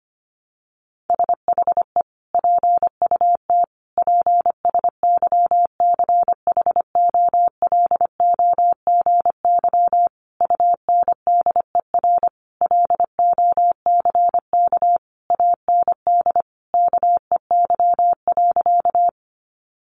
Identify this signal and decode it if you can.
Morse code, heard as a keyed tone, about 25 wpm, 700 Hz. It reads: SHE PUT PSYCHOLOGY UNDER LOCK AND KEY.
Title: Sentences from Top 1400 Words 25wpm